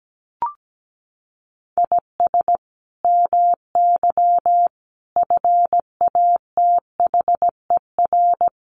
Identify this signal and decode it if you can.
Morse code, heard as a keyed tone, about 17 wpm, 700 Hz. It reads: IS MY FATHER
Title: Sentences from Top 300 Words 17wpm